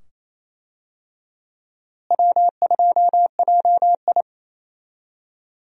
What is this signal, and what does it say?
Morse code, heard as a keyed tone, about 28 wpm, 700 Hz. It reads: W2JI
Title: USA Call Signs - 4 Characters - Repeated 2x - 28wpm